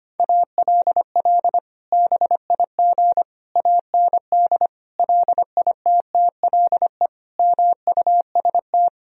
Morse code, heard as a keyed tone, about 25 wpm, 700 Hz. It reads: ALL BIG AND LITTLE MUST
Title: Sentences from Top 1000 Words - Review Part 1 - 25wpm